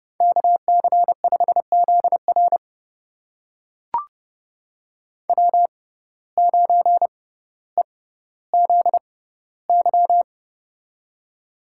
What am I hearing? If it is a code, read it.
Morse code, heard as a keyed tone, about 30 wpm, spaced as for 15 wpm, 700 Hz. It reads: KC5ZR W9EZY
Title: USA Call Signs - 5 Characters - 6x Character Spacing - 30wpm